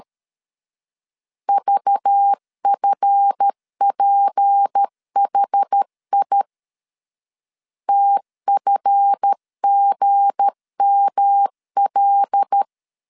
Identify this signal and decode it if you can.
Morse code, heard as a keyed tone, about 13 wpm, 800 Hz. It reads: VFPHI TFGML
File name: LEZIONI / 7 GF9Z / 7 GF 12wpm 2122